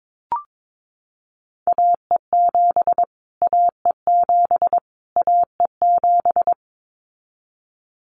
Morse code, heard as a keyed tone, about 22 wpm, 700 Hz. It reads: AE7 AE7 AE7
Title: Callsign Prefixes for Top 25 DXCC - Repeated 3x - 22wpm